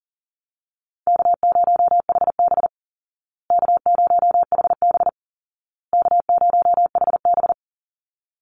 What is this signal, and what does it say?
Morse code, heard as a keyed tone, about 40 wpm, 700 Hz. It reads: K0HB K0HB K0HB
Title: Call Signs Repeated 2x - 40wpm